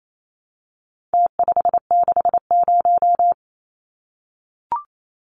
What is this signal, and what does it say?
Morse code, heard as a keyed tone, about 28 wpm, 700 Hz. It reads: T560
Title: Numbers - Contest Style - 4-Digit 28wpm